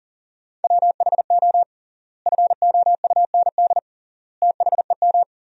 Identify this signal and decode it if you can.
Morse code, heard as a keyed tone, about 40 wpm, 700 Hz. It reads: WHO FOUND THEM
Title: 3x Word-Spacing - Sentences from Top 300 Words - 40wpm